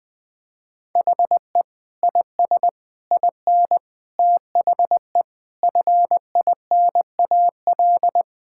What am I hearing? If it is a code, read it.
Morse code, heard as a keyed tone, about 21 wpm, 700 Hz. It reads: HE IS IN THE FINAL
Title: Sentences with 8-Word Phrases - 20wpm